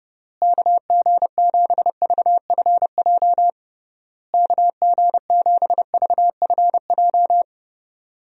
Morse code, heard as a keyed tone, about 30 wpm, 700 Hz. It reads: KG7VFJ KG7VFJ
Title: USA Call Signs - 6 Characters - Repeated 3x - 30wpm